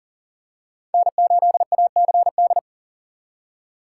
Morse code, heard as a keyed tone, about 40 wpm, 700 Hz. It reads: N8ACD
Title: USA Call Signs - 3x Character Spacing - 40wpm